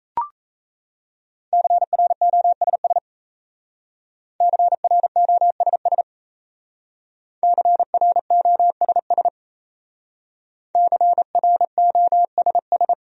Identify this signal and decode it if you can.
Morse code, heard as a keyed tone, about 35 wpm, 700 Hz. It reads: CROSS CROSS CROSS CROSS